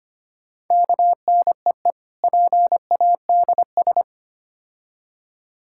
Morse code, heard as a keyed tone, about 27 wpm, 700 Hz. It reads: KNEE PADS
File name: practice / 25wpm / Difficult Words - Speed-Racing with Hints - 25wpm